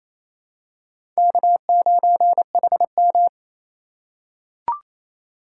Morse code, heard as a keyed tone, about 28 wpm, 700 Hz. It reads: K9HM
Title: USA Call Signs - 4 Characters - 8x Character Spacing - 28wpm